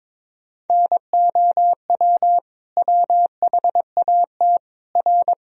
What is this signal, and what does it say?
Morse code, heard as a keyed tone, about 22 wpm, 700 Hz. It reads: NOW WHAT R
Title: Sentences from Top 100 Words and Abbreviations 22wpm